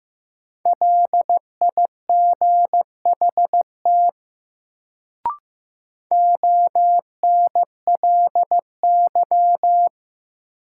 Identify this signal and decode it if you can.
Morse code, heard as a keyed tone, about 15 wpm, 700 Hz. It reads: LIGHT ONLY
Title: Sentences from Top 200 Words - New Word Drill - 15wpm